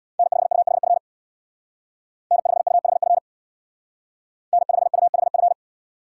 Morse code, heard as a keyed tone, about 64 wpm, 700 Hz. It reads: N5UHF N5UHF N5UHF